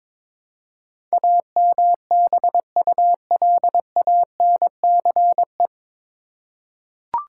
Morse code, heard as a keyed tone, about 22 wpm, 700 Hz. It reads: AMBULANCE
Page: Morse Code Ninja practice 22wpm